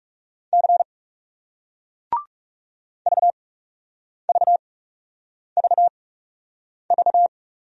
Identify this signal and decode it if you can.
Morse code, heard as a keyed tone, about 41 wpm, spaced as for 8 wpm, 700 Hz. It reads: C VVVV